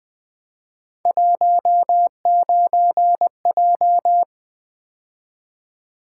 Morse code, heard as a keyed tone, about 20 wpm, 700 Hz. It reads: 19J
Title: Three Letter-Number Combinations - 20wpm